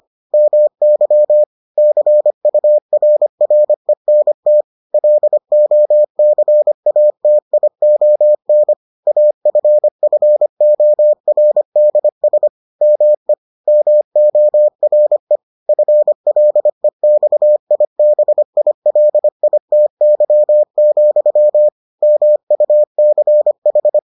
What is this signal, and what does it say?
Morse code, heard as a keyed tone, about 25 wpm, 600 Hz. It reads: MY CURRENT LOCATION AFFORDS ME MORE FLEXIBILITY, MUCH